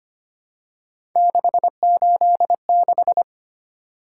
Morse code, heard as a keyed tone, about 25 wpm, 700 Hz. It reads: 686
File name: Three Letter-Number Combinations - 25wpm